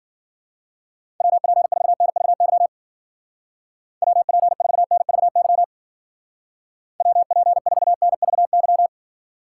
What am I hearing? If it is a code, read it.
Morse code, heard as a keyed tone, about 49 wpm, 700 Hz. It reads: WP4NVY WP4NVY WP4NVY
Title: Speed Racing - USA 2x3 Call Signs 40wpm